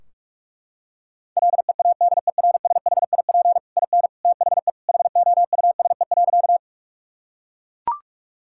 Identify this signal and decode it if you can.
Morse code, heard as a keyed tone, about 41 wpm, 700 Hz. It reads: LEADERSHIP IN THE HOUSE.
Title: Sentences from Top 1600 Words 45wpm